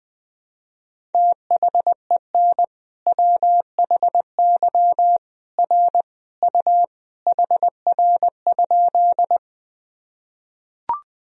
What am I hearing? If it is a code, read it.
Morse code, heard as a keyed tone, about 20 wpm, 700 Hz. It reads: THEN WHY R U HR?
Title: Sentences from Top 200 Words and Abbreviations 20wpm